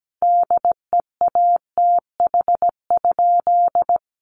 Morse code, heard as a keyed tone, about 17 wpm, 700 Hz. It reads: DEATH?
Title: Sentences from Top 1100 Words 17wpm